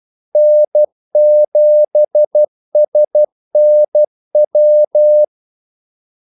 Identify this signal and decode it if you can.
Morse code, heard as a keyed tone, about 12 wpm, 600 Hz. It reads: N7SNW
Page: 2 ekstra ordmellemrum mellem kaldesignalerne